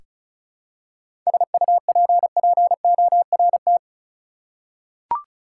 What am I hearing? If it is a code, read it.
Morse code, heard as a keyed tone, about 36 wpm, 700 Hz. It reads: SUPPORT
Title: Seven Letter Words - 35wpm